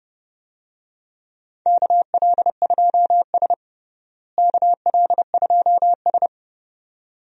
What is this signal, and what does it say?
Morse code, heard as a keyed tone, about 30 wpm, 700 Hz. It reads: KL2S KL2S
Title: USA Call Signs - 4 Characters - Repeated 2x - 30wpm